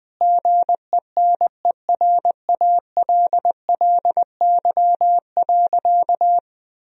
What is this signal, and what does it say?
Morse code, heard as a keyed tone, about 20 wpm, 700 Hz. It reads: GENERALLY.
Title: Sentences from Top 1000 Words - Review Part 4 - 20wpm